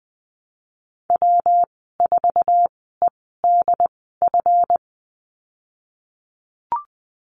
Morse code, heard as a keyed tone, about 20 wpm, spaced as for 15 wpm, 700 Hz. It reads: W4EDF
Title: 154 Lesson 25g - Letter p - Call Signs 20wpm